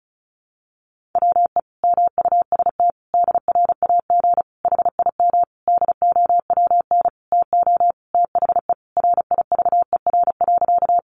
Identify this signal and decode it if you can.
Morse code, heard as a keyed tone, about 34 wpm, 700 Hz. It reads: WE MUST DRAG HIM DOWN TO THE RIVER.
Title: Sentences from Top 1500 Words 35wpm